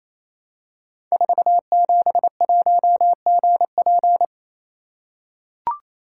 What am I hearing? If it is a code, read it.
Morse code, heard as a keyed tone, about 28 wpm, 700 Hz. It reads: 471GP
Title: Code Groups - 5 Letter:Numbers - 28wpm